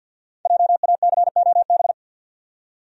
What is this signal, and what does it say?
Morse code, heard as a keyed tone, about 52 wpm, 700 Hz. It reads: JACOB
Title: USA Common Names - 50wpm